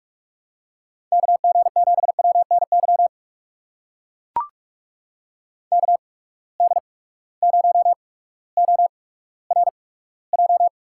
Morse code, heard as a keyed tone, about 44 wpm, spaced as for 19 wpm, 700 Hz. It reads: KG7WNY KD0ORJ
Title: USA Call Signs - 6 Characters - 8x Character Spacing - 45wpm